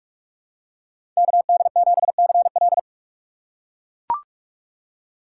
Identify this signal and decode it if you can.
Morse code, heard as a keyed tone, about 42 wpm, 700 Hz. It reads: KD7CL
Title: USA Call Signs - 5 Characters - 2x Character Spacing - 45wpm